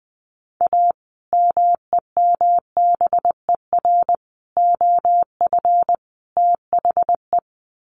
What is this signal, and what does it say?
Morse code, heard as a keyed tone, about 20 wpm, 700 Hz. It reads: A MEMBER OF THE
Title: Sets of 4 Words 20wpm